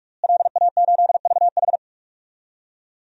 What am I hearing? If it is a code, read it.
Morse code, heard as a keyed tone, about 43 wpm, 700 Hz. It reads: LA8VH
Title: DXCC-100 Call Signs - 5 Characters - Repeated 2x - 45wpm